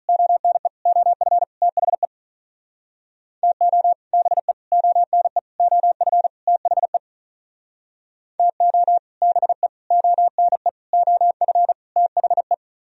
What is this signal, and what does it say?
Morse code, heard as a keyed tone, about 42 wpm, 700 Hz. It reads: ONE OF THE TO BE ONE OF THE TO BE ONE OF THE
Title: Speed-Racing - Sets of 5 Words 35wpm